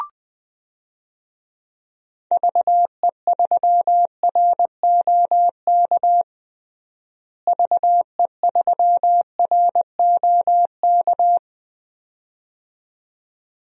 Morse code, heard as a keyed tone, about 21 wpm, 700 Hz. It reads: VE3ROK VE3ROK
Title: DXCC-100 Call Signs - 6 Characters - Repeated 2x - 20wpm